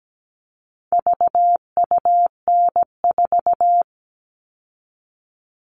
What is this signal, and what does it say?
Morse code, heard as a keyed tone, about 17 wpm, 700 Hz. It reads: VUN4
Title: All Cut Numbers - 4-Digit 17wpm